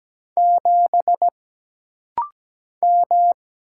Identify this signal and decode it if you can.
Morse code, heard as a keyed tone, about 17 wpm, 700 Hz. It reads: 7 M
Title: Single Letter-Number - Rapid-Fire 17wpm